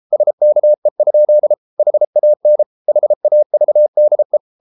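Morse code, heard as a keyed tone, about 33 wpm, 600 Hz. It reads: SKE? HAN HAVDE
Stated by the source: Morse code at 33 wpm, QRQ.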